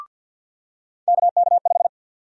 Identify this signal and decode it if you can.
Morse code, heard as a keyed tone, about 48 wpm, 700 Hz. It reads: KK5
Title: USA Call Sign Prefixes - 50wpm